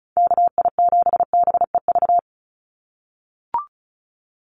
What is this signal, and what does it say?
Morse code, heard as a keyed tone, about 36 wpm, 700 Hz. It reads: KI7BEV
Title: USA Call Signs 35wpm